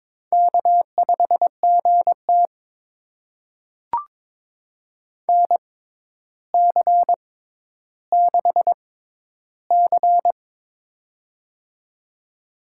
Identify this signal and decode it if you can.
Morse code, heard as a keyed tone, about 21 wpm, spaced as for 11 wpm, 700 Hz. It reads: K5GT NC6C
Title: USA Call Signs - 6x Character Spacing - 22wpm